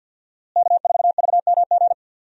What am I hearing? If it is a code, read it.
Morse code, heard as a keyed tone, about 47 wpm, 700 Hz. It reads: K4VMG
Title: USA Call Signs - 5 Characters - Repeated 2x - 50wpm